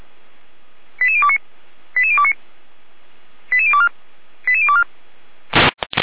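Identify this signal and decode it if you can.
Five-tone selective call (ZVEI), heard as quick runs of five tones, tones of about 70 ms. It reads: 88029 88029 88024 88024
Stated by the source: ZVEI five-tone call received by SDR, from feuerwehr-orlamuende.org fire brigade audio archive.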